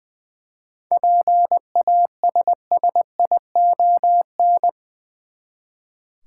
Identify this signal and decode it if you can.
Morse code, heard as a keyed tone, about 20 wpm, 700 Hz. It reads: PASSION
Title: Seven Letter Words - 20wpm